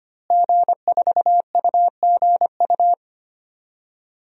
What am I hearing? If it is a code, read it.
Morse code, heard as a keyed tone, about 25 wpm, 700 Hz. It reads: G4UGU
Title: DXCC-100 Call Signs - 5 Characters - Repeated 2x - 25wpm